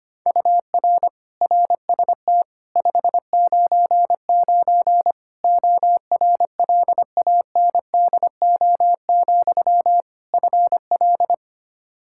Morse code, heard as a keyed tone, about 25 wpm, 700 Hz. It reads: UR RST 599 ORLANDO, FL
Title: Practice QSO - Exchange 1 - Beginning - 25wpm